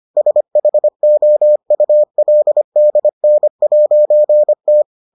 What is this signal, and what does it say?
Morse code, heard as a keyed tone, about 25 wpm, 600 Hz. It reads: SHOULDN'T